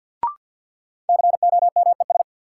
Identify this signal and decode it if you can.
Morse code, heard as a keyed tone, about 53 wpm, 700 Hz. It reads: COMES
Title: Sets of 3 Words - New Word Drill - 50wpm